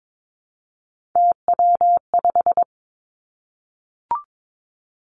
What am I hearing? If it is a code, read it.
Morse code, heard as a keyed tone, about 22 wpm, 700 Hz. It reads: TW5
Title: Three Challenging Letter-Number Combinations 22wpm